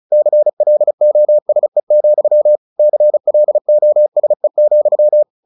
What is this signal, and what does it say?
Morse code, heard as a keyed tone, about 34 wpm, 600 Hz. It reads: CLOSE, CLOSE,
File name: the old man and the sea 2 35wpm